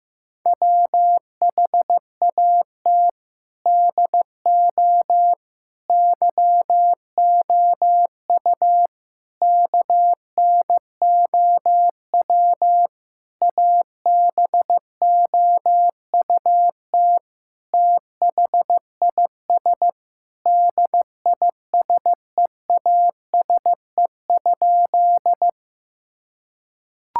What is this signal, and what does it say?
Morse code, heard as a keyed tone, about 16 wpm, 700 Hz. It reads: WHAT DO YOU KNOW ABOUT THIS DISEASE?
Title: Sentences from Top 1000 Words - Review Part 6 - 15wpm